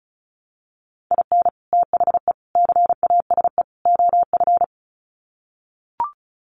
Morse code, heard as a keyed tone, about 34 wpm, 700 Hz. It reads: IN THE CASE OF
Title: Sets of 4 Words 35wpm